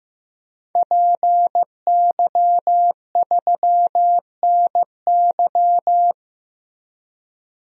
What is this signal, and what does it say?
Morse code, heard as a keyed tone, about 16 wpm, 700 Hz. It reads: PY3NY
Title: DXCC-100 Call Signs - 5 Characters - Repeated 2x - 15wpm